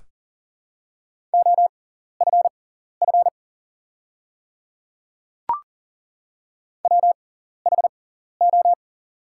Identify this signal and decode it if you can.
Morse code, heard as a keyed tone, about 40 wpm, spaced as for 14 wpm, 690 Hz. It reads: OFF WHO
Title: 6x Character Spacing - Top 500 Words - 40wpm